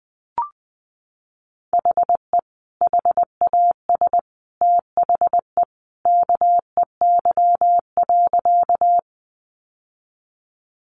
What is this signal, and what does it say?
Morse code, heard as a keyed tone, about 20 wpm, 700 Hz. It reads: HE HAS THE KEY.
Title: Sentences from Top 600 Words 20wpm